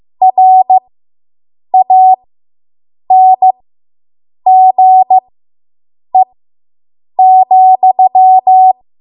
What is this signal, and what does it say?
Morse code, heard as a keyed tone, about 15 wpm, spaced as for 7 wpm, 750 Hz. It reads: RANGE,